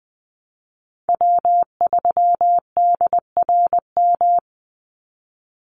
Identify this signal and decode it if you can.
Morse code, heard as a keyed tone, about 20 wpm, 700 Hz. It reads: W3DRM
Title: USA Call Signs 20wpm